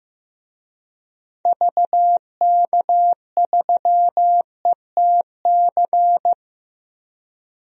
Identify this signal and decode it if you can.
Morse code, heard as a keyed tone, about 15 wpm, 700 Hz. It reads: VK3ETC
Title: DXCC-100 Call Signs - 6 Characters - Repeated 3x - 15wpm